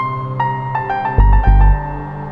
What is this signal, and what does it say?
There is no code.